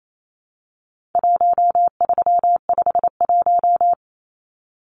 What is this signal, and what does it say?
Morse code, heard as a keyed tone, about 28 wpm, 700 Hz. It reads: 1351
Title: Numbers - 4-Digit 28wpm